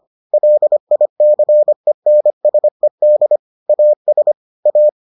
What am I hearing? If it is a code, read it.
Morse code, heard as a keyed tone, about 25 wpm, 600 Hz. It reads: LICENSED AS A